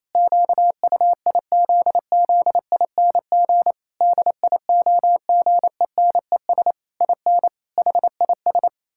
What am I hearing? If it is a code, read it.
Morse code, heard as a keyed tone, about 28 wpm, 700 Hz. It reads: QUIZZING DIOGENES IN HIS
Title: Pangram Challenge v2 - 28wpm